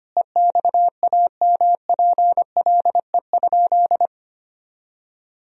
Morse code, heard as a keyed tone, about 25 wpm, 700 Hz. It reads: EXAMPLE?
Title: Sentences from Top 1000 Words - Review Part 1 - 25wpm